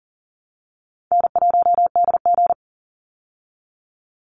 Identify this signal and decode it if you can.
Morse code, heard as a keyed tone, about 40 wpm, 700 Hz. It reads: N1DG
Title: Call Signs Repeated 3x - 40wpm